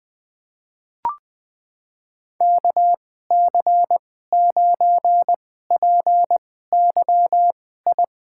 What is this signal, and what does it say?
Morse code, heard as a keyed tone, about 20 wpm, spaced as for 15 wpm, 700 Hz. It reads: KC9PYI
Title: USA Call Signs - 6 Characters - 2x Character Spacing - 20wpm